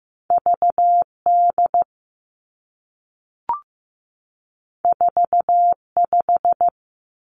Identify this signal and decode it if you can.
Morse code, heard as a keyed tone, about 15 wpm, 700 Hz. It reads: V D 4 5
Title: Two Challenging Letter-Number Combinations 15wpm